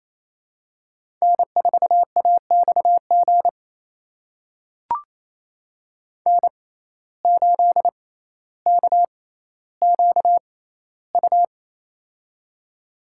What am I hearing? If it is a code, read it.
Morse code, heard as a keyed tone, about 28 wpm, spaced as for 14 wpm, 700 Hz. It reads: N4AXG N8KQU
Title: USA Call Signs - 5 Characters - 6x Character Spacing - 28wpm